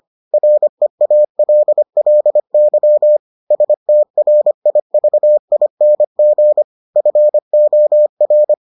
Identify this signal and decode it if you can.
Morse code, heard as a keyed tone, about 25 wpm, 600 Hz. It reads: REALLY STRIVING FOR